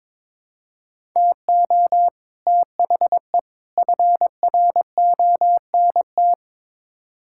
Morse code, heard as a keyed tone, about 22 wpm, 700 Hz. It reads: TO THE FRONT
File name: practice / 22wpm / Build A Phrase - 5-Word Phrases - 22wpm